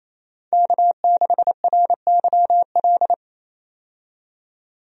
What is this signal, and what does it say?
Morse code, heard as a keyed tone, about 28 wpm, 700 Hz. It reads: K6RYL
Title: USA Call Signs - 5 Characters - 28wpm